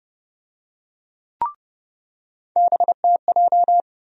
Morse code, heard as a keyed tone, about 30 wpm, 700 Hz. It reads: BTJ
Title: USA Call Sign Postfixes - 3 Letters 30wpm